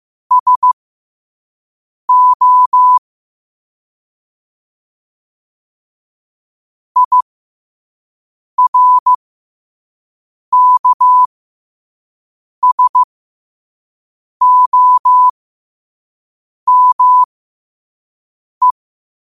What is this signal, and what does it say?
Morse code, heard as a keyed tone, about 15 wpm, spaced as for 5 wpm, 1000 Hz. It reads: SO IRKSOME
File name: qotdfiles / 5wpm